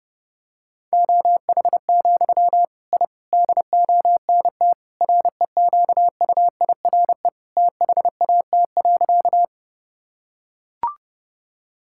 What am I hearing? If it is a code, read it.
Morse code, heard as a keyed tone, about 28 wpm, 700 Hz. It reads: OH, I DONT REQUIRE THAT.
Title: Sentences from Top 1000 Words - Review Part 6 - 30wpm